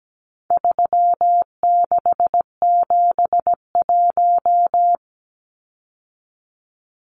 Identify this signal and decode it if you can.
Morse code, heard as a keyed tone, about 17 wpm, 700 Hz. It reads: 3671
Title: Numbers - 4-Digit 17wpm